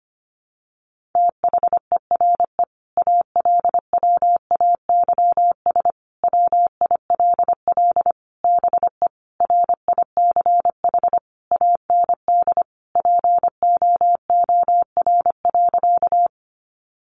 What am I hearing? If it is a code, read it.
Morse code, heard as a keyed tone, about 25 wpm, 700 Hz. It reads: THERE ALWAYS WILL BE RICH AND POOR.
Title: Sentences from Top 1100 Words 25wpm